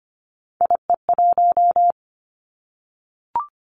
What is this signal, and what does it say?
Morse code, heard as a keyed tone, about 25 wpm, 700 Hz. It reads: IE1
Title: Three Challenging Letter-Number Combinations 25wpm